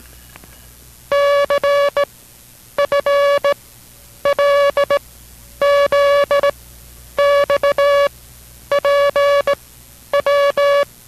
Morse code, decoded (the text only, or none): CFLZXPW